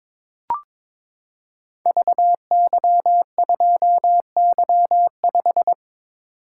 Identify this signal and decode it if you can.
Morse code, heard as a keyed tone, about 23 wpm, 700 Hz. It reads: VY2Y5
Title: Code Groups - 5 Letter:Numbers - 22wpm